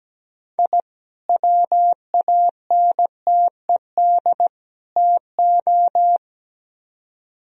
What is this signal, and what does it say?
Morse code, heard as a keyed tone, about 20 wpm, 700 Hz. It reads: I WANTED TO
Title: Speed-Racing - Sets of 3 Words 17wpm